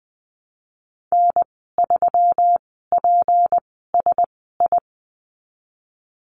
Morse code, heard as a keyed tone, about 19 wpm, spaced as for 14 wpm, 700 Hz. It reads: N3PSI
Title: 207 Lesson 33g - Letter j - Call Signs 20wpm